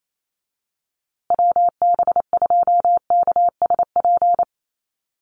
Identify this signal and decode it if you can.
Morse code, heard as a keyed tone, about 28 wpm, 700 Hz. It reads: WB2KSP
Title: USA Call Signs Repeated 3x 28wpm